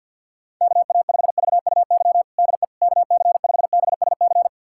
Morse code, heard as a keyed tone, about 46 wpm, 700 Hz. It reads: KA5VUY DE KC5BSC
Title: USA Callsign DE Callsign - 6 Characters - 50wpm